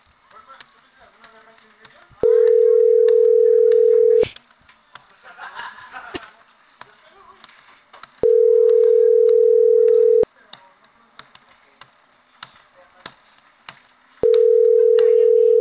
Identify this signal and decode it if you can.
Ringback tone (North America), 440 + 480 Hz.